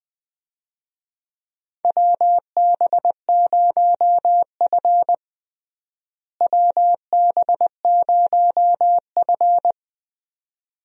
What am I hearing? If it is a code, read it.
Morse code, heard as a keyed tone, about 20 wpm, 700 Hz. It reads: WB0F WB0F
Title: USA Call Signs - 4 Characters - Repeated 2x - 20wpm